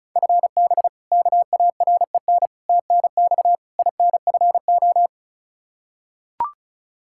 Morse code, heard as a keyed tone, about 34 wpm, 700 Hz. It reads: FB KAREN TNX INFO
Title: K1USN SST QSO Practice 35wpm